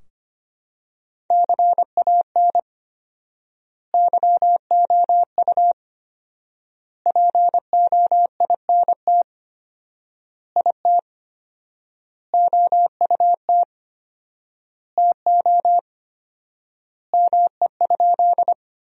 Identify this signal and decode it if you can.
Morse code, heard as a keyed tone, about 25 wpm, 700 Hz. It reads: CAN YOU POINT IT OUT TO ME?
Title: 4x Word-Spacing - Sentences from Top 200 Words - 25wpm